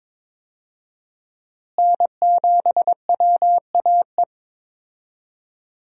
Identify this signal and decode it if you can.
Morse code, heard as a keyed tone, about 22 wpm, 700 Hz. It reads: N7WAE
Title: USA Call Signs - 5 Characters - Repeated 2x - 22wpm